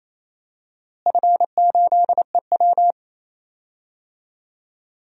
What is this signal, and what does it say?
Morse code, heard as a keyed tone, about 28 wpm, 700 Hz. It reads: F8EW
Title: DXCC-100 Call Signs - 4 Characters - Repeated 3x - 28wpm